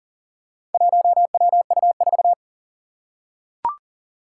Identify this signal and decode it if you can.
Morse code, heard as a keyed tone, about 40 wpm, 700 Hz. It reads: 1WU4
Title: Four Challenging Letter-Number Combinations 40wpm